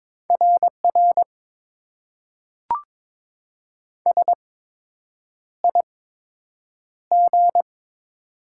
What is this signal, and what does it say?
Morse code, heard as a keyed tone, about 22 wpm, spaced as for 8 wpm, 700 Hz. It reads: RR SIG
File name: Top 90 QSO Elements - 8x Character Spacing - 22wpm